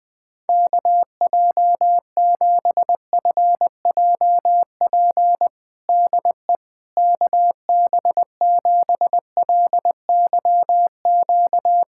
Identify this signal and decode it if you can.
Morse code, heard as a keyed tone, about 20 wpm, 700 Hz. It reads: KJ7FJP DE KB7LYQ
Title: USA Callsign DE Callsign - 6 Characters - 20wpm